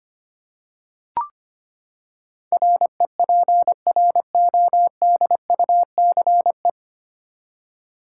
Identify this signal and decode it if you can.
Morse code, heard as a keyed tone, about 25 wpm, 700 Hz. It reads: REPRODUCE
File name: Nine Letter Words - 25wpm